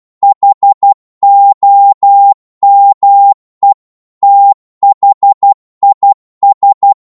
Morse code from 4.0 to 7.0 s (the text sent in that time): THIS